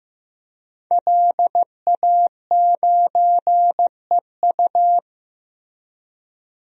Morse code, heard as a keyed tone, about 15 wpm, 700 Hz. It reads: LA9EU
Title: DXCC-100 Call Signs - 5 Characters - Repeated 2x - 15wpm